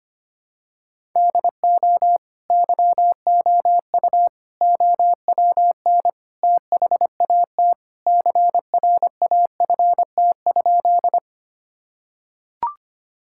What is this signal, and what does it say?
Morse code, heard as a keyed tone, about 25 wpm, 700 Hz. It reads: DO YOU OWN THAT CRAFT?
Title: Sentences from Top 1000 Words - Review Part 6 - 25wpm